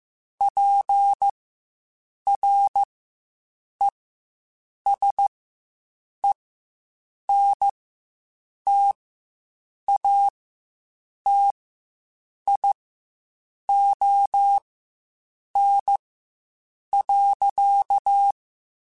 Morse code, heard as a keyed tone, about 15 wpm, spaced as for 7 wpm, 800 Hz. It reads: PRESENTATION.
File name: Quotes Spaced / Quotes Spaced-mp3s / Persperation sp @10